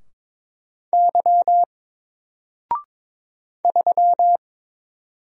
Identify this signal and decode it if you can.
Morse code, heard as a keyed tone, about 22 wpm, 700 Hz. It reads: Y 3
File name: Single-Letter-Number-22wpm